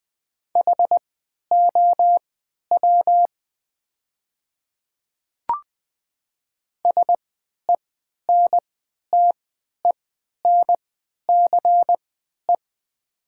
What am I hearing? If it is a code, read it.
Morse code, heard as a keyed tone, about 20 wpm, spaced as for 11 wpm, 700 Hz. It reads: HOW SENTENCE
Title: 3x Character Spacing - Top 200 Words - 20wpm